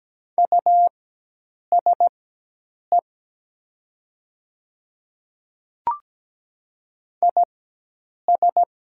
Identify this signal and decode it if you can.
Morse code, heard as a keyed tone, about 17 wpm, spaced as for 8 wpm, 700 Hz. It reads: USE IS
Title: 4x Character Spacing - Top 100 Words - 17wpm